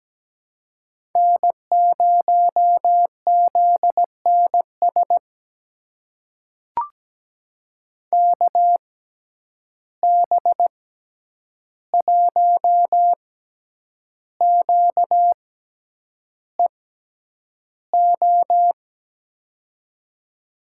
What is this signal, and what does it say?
Morse code, heard as a keyed tone, about 17 wpm, spaced as for 8 wpm, 700 Hz. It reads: N0ZNS KB1QEO
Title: USA Call Signs - 6x Character Spacing - 17wpm